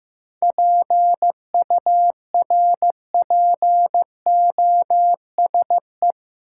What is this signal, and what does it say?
Morse code, heard as a keyed tone, about 15 wpm, 700 Hz. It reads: PURPOSE
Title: Sentences from Top 800 Words - New Word Drill - 15wpm